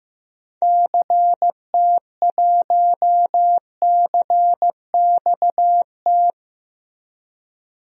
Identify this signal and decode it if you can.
Morse code, heard as a keyed tone, about 16 wpm, 700 Hz. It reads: CT1CXT
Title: DXCC-100 Call Signs - 6 Characters - Repeated 3x - 15wpm